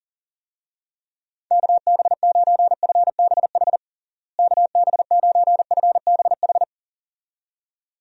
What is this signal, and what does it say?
Morse code, heard as a keyed tone, about 40 wpm, 700 Hz. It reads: KB9FBH KB9FBH
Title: USA Call Signs - 6 Characters - Repeated 2x - 40wpm